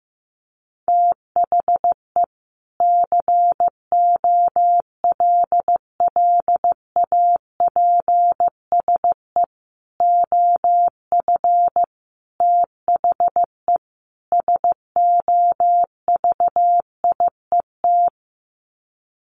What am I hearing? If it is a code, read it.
Morse code, heard as a keyed tone, about 15 wpm, 700 Hz. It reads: THE COLLAPSE OF THE SOVIET
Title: Sets of 5 Words Encore 15wpm